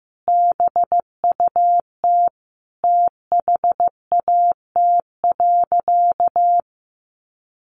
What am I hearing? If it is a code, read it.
Morse code, heard as a keyed tone, about 15 wpm, 700 Hz. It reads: BUT THAT.
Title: Sentences from Top 1000 Words 15wpm